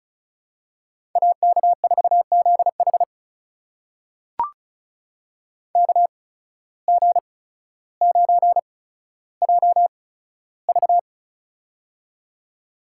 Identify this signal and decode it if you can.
Morse code, heard as a keyed tone, about 34 wpm, spaced as for 15 wpm, 700 Hz. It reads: AK4ZH KG9JV
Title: USA Call Signs - 5 Characters - 8x Character Spacing - 35wpm